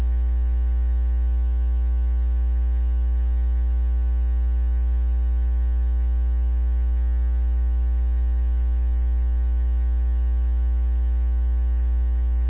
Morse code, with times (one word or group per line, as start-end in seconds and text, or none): none